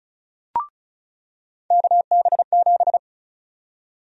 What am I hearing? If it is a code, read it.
Morse code, heard as a keyed tone, about 34 wpm, 700 Hz. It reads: KB7
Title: USA Call Sign Prefixes - 35wpm